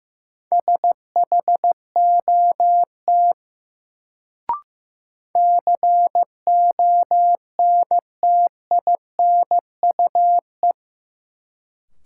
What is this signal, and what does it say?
Morse code, heard as a keyed tone, about 15 wpm, 700 Hz. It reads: SHOT CONTINUE
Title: Sentences from Top 1100 Words - New Word Drill Alt - 15wpm